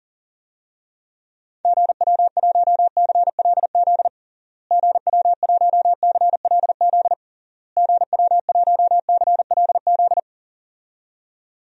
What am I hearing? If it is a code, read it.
Morse code, heard as a keyed tone, about 40 wpm, 700 Hz. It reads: GW1CLZ GW1CLZ GW1CLZ
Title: DXCC-100 Call Signs - 6 Characters - Repeated 3x - 40wpm